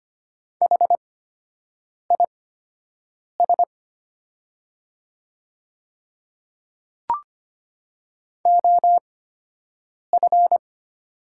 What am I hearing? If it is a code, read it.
Morse code, heard as a keyed tone, about 25 wpm, spaced as for 7 wpm, 700 Hz. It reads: HIS OF
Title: 8x Character Spacing - Top 200 Words - 25wpm